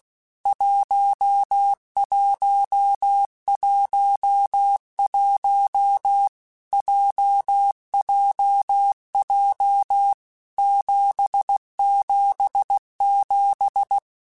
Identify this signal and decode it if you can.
Morse code, heard as a keyed tone, about 16 wpm, 800 Hz. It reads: 1111 JJJ 777
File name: Basic Letter Groups By7Groups / JZ17 / JZ17@15